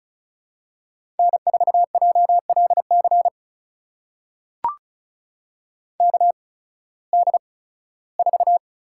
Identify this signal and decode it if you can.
Morse code, heard as a keyed tone, about 34 wpm, spaced as for 18 wpm, 700 Hz. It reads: N4JLC KD4
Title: USA Call Signs - 5 Characters - 8x Character Spacing - 35wpm